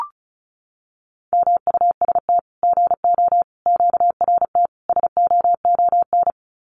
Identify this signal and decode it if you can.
Morse code, heard as a keyed tone, about 34 wpm, 700 Hz. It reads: MUST GO QRT SOON
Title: Common QSO Phrases 35wpm